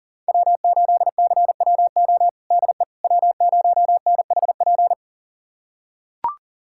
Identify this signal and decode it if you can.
Morse code, heard as a keyed tone, about 37 wpm, 700 Hz. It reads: W8CWO DE W0NHP
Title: USA Callsign DE Callsign - 5 Characters - 40wpm